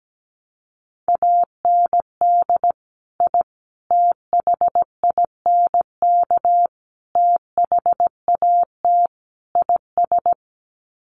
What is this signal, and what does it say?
Morse code, heard as a keyed tone, about 17 wpm, 700 Hz. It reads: AND I THINK THAT IS
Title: Sets of 5 Words Encore2 17wpm